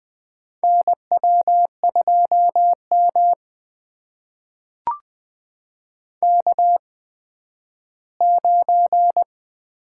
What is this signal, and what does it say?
Morse code, heard as a keyed tone, about 20 wpm, spaced as for 12 wpm, 700 Hz. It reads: NW2M K9
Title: USA Call Signs - 4 Characters - 8x Character Spacing - 20wpm